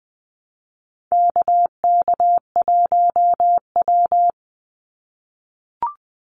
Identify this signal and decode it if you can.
Morse code, heard as a keyed tone, about 20 wpm, 700 Hz. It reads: KK1W
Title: NA SOTA Activator Callsigns 20wpm